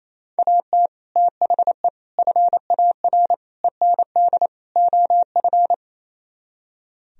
Morse code, heard as a keyed tone, about 35 wpm, 700 Hz. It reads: AT THE FAR END OF